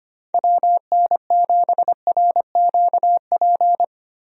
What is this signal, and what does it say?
Morse code, heard as a keyed tone, about 25 wpm, 700 Hz. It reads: WN7RQP
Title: USA Call Signs - 6 Characters - Repeated 2x - 25wpm